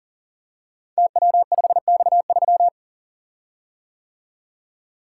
Morse code, heard as a keyed tone, about 39 wpm, 700 Hz. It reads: TW5X3
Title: Code Groups - 5 Letter:Numbers - 40wpm